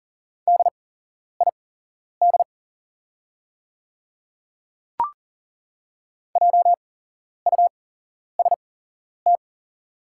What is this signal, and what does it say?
Morse code, heard as a keyed tone, about 38 wpm, spaced as for 11 wpm, 700 Hz. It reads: DID JUST
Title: 8x Character Spacing - Top 200 Words - 40wpm